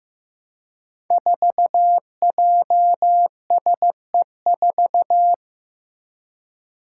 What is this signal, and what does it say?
Morse code, heard as a keyed tone, about 15 wpm, 700 Hz. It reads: 4JSE4
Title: Five Challenging Letter-Number Combinations 15wpm